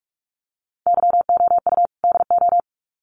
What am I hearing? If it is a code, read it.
Morse code, heard as a keyed tone, about 44 wpm, 700 Hz. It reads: YOU DO
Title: Sets of 2 Words 45wpm